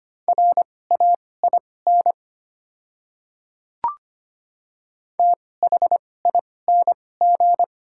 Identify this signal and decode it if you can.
Morse code, heard as a keyed tone, about 25 wpm, spaced as for 18 wpm, 700 Hz. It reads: RAIN THING
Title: 2x Character Spacing - Top 300 Words - 25wpm